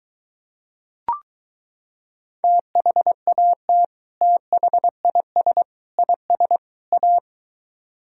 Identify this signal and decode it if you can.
Morse code, heard as a keyed tone, about 23 wpm, 700 Hz. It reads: THAT THIS IS A